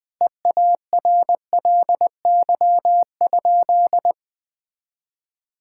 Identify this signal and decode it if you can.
Morse code, heard as a keyed tone, about 21 wpm, 700 Hz. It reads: EARLY?
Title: Sentences with 5-Word Phrases - 20wpm